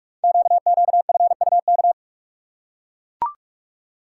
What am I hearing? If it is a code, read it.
Morse code, heard as a keyed tone, about 44 wpm, 700 Hz. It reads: QQFUK
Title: Code Groups - 5 Letters - 45wpm